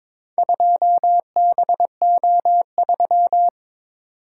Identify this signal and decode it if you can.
Morse code, heard as a keyed tone, about 22 wpm, 700 Hz. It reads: 2BO3
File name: practice / 22wpm / Four Letter-Number Combinations - 22wpm